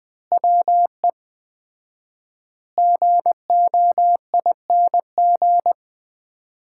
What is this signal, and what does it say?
Morse code, heard as a keyed tone, about 20 wpm, 700 Hz. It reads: WE GOING
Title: 4x Word-Spacing - Sets of 5 Words - 20wpm